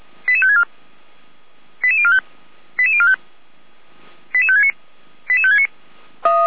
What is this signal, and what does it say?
Five-tone selective call (ZVEI), heard as quick runs of five tones, tones of about 70 ms. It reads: 88564 88045 88045 88569 88569